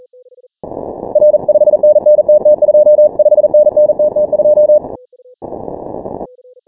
Morse code, heard as a keyed tone, about 40 wpm, 600 Hz. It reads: R 5NN TT2 5NN TT2
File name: final / W1UJ-168-140-MorseRunner